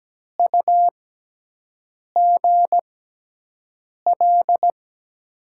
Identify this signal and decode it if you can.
Morse code, heard as a keyed tone, about 17 wpm, spaced as for 6 wpm, 700 Hz. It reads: UGL
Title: USA Call Signs - 6 Characters - 6x Character Spacing - 17wpm